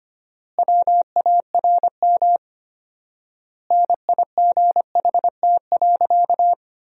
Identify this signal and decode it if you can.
Morse code, heard as a keyed tone, about 25 wpm, 700 Hz. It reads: WARM NIGHT.
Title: 4x Word-Spacing - Sentences from Top 500 Words - 25wpm